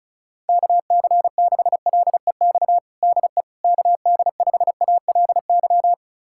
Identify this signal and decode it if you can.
Morse code, heard as a keyed tone, about 36 wpm, 700 Hz. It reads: KC6LEX DE KD5ALY
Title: USA Callsign DE Callsign 35wpm